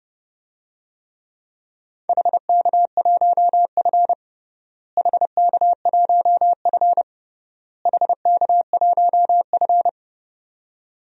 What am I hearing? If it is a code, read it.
Morse code, heard as a keyed tone, about 30 wpm, 700 Hz. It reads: HK1F HK1F HK1F
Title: DXCC-100 Call Signs - 4 Characters - Repeated 3x - 30wpm